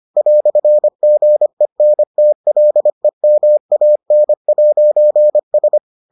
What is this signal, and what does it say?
Morse code, heard as a keyed tone, about 25 wpm, 600 Hz. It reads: "GENTLEMAN'S